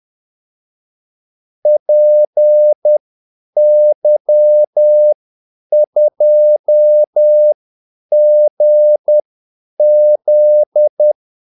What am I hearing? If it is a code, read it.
Morse code, heard as a keyed tone, about 10 wpm, spaced as for 8 wpm, 600 Hz. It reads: PY2GZ
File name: Callsigns-10-8wpm